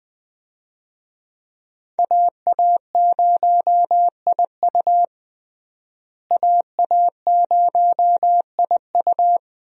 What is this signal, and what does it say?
Morse code, heard as a keyed tone, about 21 wpm, 700 Hz. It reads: AA0IU AA0IU
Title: USA Call Signs - 5 Characters - Repeated 2x - 20wpm